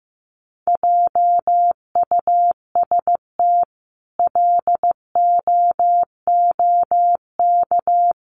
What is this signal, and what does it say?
Morse code, heard as a keyed tone, about 15 wpm, 700 Hz. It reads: JUST LOOK
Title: Sentences from Top 1000 Words 15wpm